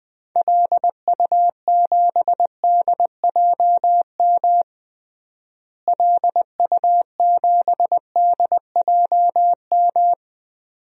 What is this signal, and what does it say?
Morse code, heard as a keyed tone, about 21 wpm, 700 Hz. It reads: LU7DJM LU7DJM
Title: DXCC-100 Call Signs - 6 Characters - Repeated 2x - 20wpm